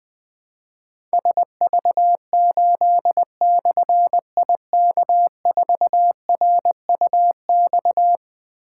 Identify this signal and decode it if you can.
Morse code, heard as a keyed tone, about 20 wpm, 700 Hz. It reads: SV8/IK4RUX
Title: Difficult Call Signs 20wpm